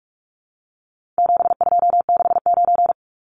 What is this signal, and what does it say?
Morse code, heard as a keyed tone, about 44 wpm, 700 Hz. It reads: 7269
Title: Numbers - 4-Digit 45wpm